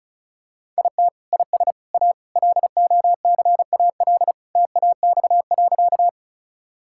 Morse code, heard as a keyed tone, about 36 wpm, 700 Hz. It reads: IT IS A LOCAL TAX.
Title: Sentences from Top 1000 Words - Review Part 5 - 35wpm